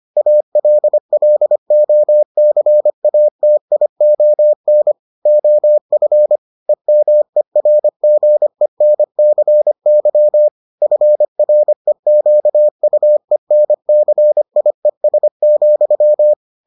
Morse code, heard as a keyed tone, about 25 wpm, 600 Hz. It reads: ALLOCATION OF EMERGENCY FREQUENCIES,